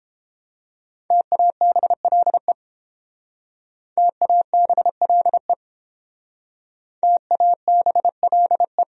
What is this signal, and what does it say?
Morse code, heard as a keyed tone, about 30 wpm, 700 Hz. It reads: TABLE TABLE TABLE